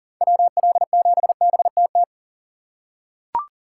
Morse code, heard as a keyed tone, about 39 wpm, 700 Hz. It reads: WL7BTT
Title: USA Call Signs - 6 Characters - 4x Character Spacing - 40wpm